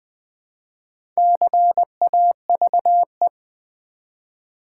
Morse code, heard as a keyed tone, about 20 wpm, 700 Hz. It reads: CAVE
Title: Build A Word - 1 Letter At A Time - 3 to 5-Letter Words - 20wpm